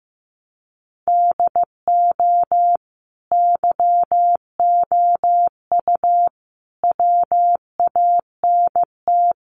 Morse code, heard as a keyed tone, about 15 wpm, 700 Hz. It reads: DO YOU WANT
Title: Sentences from Top 1100 Words 15wpm